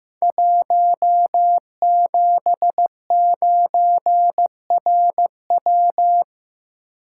Morse code, heard as a keyed tone, about 15 wpm, 700 Hz. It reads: 179RW
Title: Code Groups - 5 Letter:Numbers - 15wpm